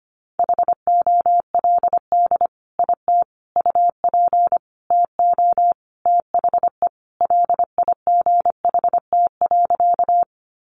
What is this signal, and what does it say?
Morse code, heard as a keyed tone, about 25 wpm, 700 Hz. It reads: HOLD IT UP TO THE LIGHT.
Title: Sentences from Top 400 Words 25wpm